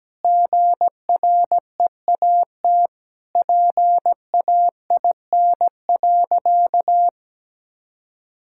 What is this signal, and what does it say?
Morse code, heard as a keyed tone, about 17 wpm, 700 Hz. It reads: GREAT PAIN.
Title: Sentences from Top 1600 Words 17wpm